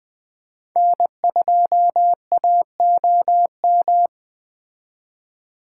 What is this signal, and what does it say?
Morse code, heard as a keyed tone, about 20 wpm, 700 Hz. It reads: N2AOM
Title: Speed Racing - USA Call Signs 17wpm